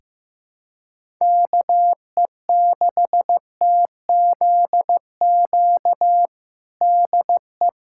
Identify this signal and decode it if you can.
Morse code, heard as a keyed tone, about 15 wpm, 700 Hz. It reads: KE6TZQ DE
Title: USA Callsign DE Callsign 15wpm